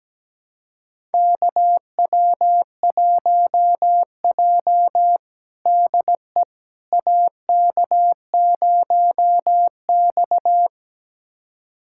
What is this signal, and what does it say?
Morse code, heard as a keyed tone, about 17 wpm, 700 Hz. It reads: KW1J DE AK0X
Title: USA Callsign DE Callsign - 4 Characters - 17wpm